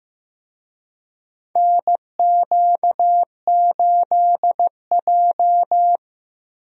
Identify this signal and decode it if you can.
Morse code, heard as a keyed tone, about 15 wpm, 700 Hz. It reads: NQ8J
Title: USA Call Signs - 4 Characters - Repeated 2x - 15wpm